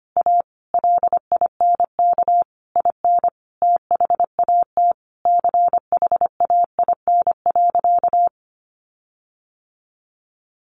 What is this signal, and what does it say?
Morse code, heard as a keyed tone, about 25 wpm, 700 Hz. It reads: A LINK IN THAT CHAIN.
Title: Sentences from Top 1500 Words 25wpm